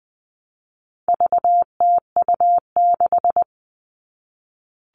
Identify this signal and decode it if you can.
Morse code, heard as a keyed tone, about 20 wpm, 700 Hz. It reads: VTU6
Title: All Cut Numbers - 4-Digit 20wpm